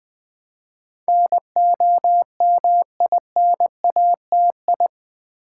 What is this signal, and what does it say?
Morse code, heard as a keyed tone, about 20 wpm, 700 Hz. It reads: NOMINATI
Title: Build A Word - 4 Letters At A Time - 10-Letter Words - 20wpm